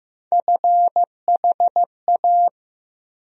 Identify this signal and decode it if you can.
Morse code, heard as a keyed tone, about 16 wpm, 700 Hz. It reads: FHA
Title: USA Call Sign Postfixes - 3 Letters 15wpm